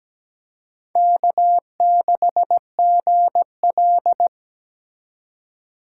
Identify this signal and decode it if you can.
Morse code, heard as a keyed tone, about 20 wpm, 700 Hz. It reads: K6GL
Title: Speed Racing - USA 1x2 Call Signs 17wpm